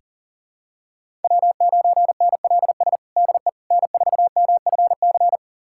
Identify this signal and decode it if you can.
Morse code, heard as a keyed tone, about 37 wpm, 700 Hz. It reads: W9NLS DE N4MFC
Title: USA Callsign DE Callsign 40wpm